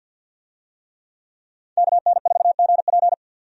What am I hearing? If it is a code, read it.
Morse code, heard as a keyed tone, about 52 wpm, 700 Hz. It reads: KN4GP
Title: USA Call Signs - 5 Characters - 50wpm